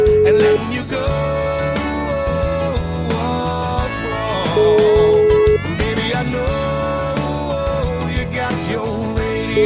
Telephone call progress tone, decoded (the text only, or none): ringback tone